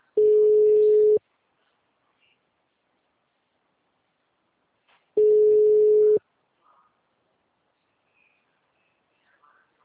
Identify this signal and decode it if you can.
Ringback tone (Europe, 425 Hz).